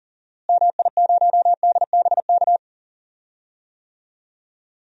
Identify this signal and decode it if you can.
Morse code, heard as a keyed tone, about 39 wpm, 700 Hz. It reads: MI0DBK
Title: DXCC-100 Call Signs - 6 Characters - Repeated 2x - 40wpm